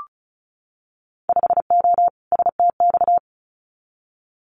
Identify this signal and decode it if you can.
Morse code, heard as a keyed tone, about 34 wpm, 690 Hz. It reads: 5O STX
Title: Winter Field Day Exchanges 35wpm